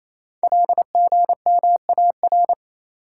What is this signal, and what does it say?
Morse code, heard as a keyed tone, about 27 wpm, 690 Hz. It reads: LGMAR
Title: Code Groups - 5 Letters - 28wpm